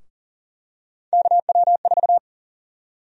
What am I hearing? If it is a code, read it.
Morse code, heard as a keyed tone, about 40 wpm, 700 Hz. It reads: KW4
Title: USA Call Sign Prefixes - 40wpm